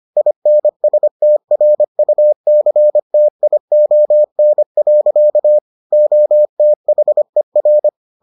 Morse code, heard as a keyed tone, about 25 wpm, 600 Hz. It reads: INSTRUCTION. OTHER